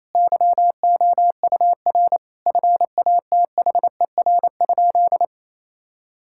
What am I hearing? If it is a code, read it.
Morse code, heard as a keyed tone, about 28 wpm, 700 Hz. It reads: YOUR FATHER?
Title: Sentences from Top 1000 Words - Review Part 5 - 28wpm